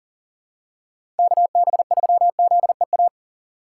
Morse code, heard as a keyed tone, about 40 wpm, 700 Hz. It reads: KB3ZEA
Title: USA Call Signs - 6 Characters - 2x Character Spacing - 40wpm